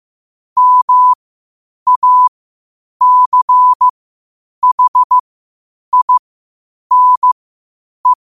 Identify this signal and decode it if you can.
Morse code, heard as a keyed tone, about 15 wpm, spaced as for 8 wpm, 1000 Hz. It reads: MACHINE
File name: qotdfiles / 8wpm